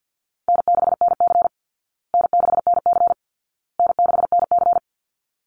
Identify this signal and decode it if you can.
Morse code, heard as a keyed tone, about 50 wpm, 700 Hz. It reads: N6NC N6NC N6NC
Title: Easy Call Signs Repeated 3x - 50wpm